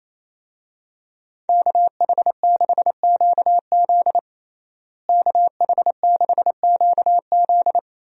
Morse code, heard as a keyed tone, about 28 wpm, 700 Hz. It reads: KH6QZ KH6QZ
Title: USA Call Signs - 5 Characters - Repeated 2x - 28wpm